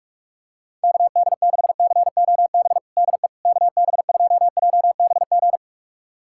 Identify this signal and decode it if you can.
Morse code, heard as a keyed tone, about 47 wpm, 690 Hz. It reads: KD6COB DE KB2JBG
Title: USA Callsign DE Callsign - 6 Characters - 45wpm